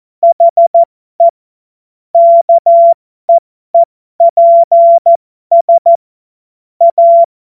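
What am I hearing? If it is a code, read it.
Morse code, heard as a keyed tone, about 14 wpm, spaced as for 12 wpm, 680 Hz. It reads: HE KEEPS A
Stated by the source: Amateur radio CW practice text made of jokes.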